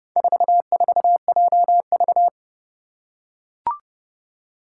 Morse code, heard as a keyed tone, about 30 wpm, 700 Hz. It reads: 44JV
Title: Four Challenging Letter-Number Combinations 30wpm